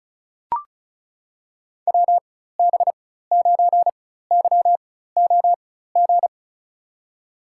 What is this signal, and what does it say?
Morse code, heard as a keyed tone, about 36 wpm, spaced as for 16 wpm, 700 Hz. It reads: WB9YOG
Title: USA Call Signs - 6 Characters - 4x Character Spacing - 35wpm